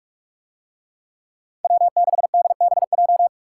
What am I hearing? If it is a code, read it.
Morse code, heard as a keyed tone, about 43 wpm, 700 Hz. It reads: W6DBJ